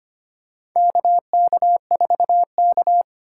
Morse code, heard as a keyed tone, about 25 wpm, 700 Hz. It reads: KK4K
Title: USA Call Signs - 4 Characters - 3x Character Spacing - 25wpm